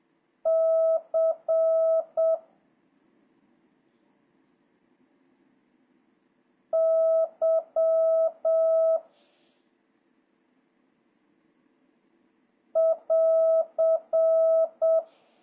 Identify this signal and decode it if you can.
Morse code, heard as a keyed tone, about 7 wpm, 1300 Hz, with noise. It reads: C Y <AR>
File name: jh8cbh / 2026cw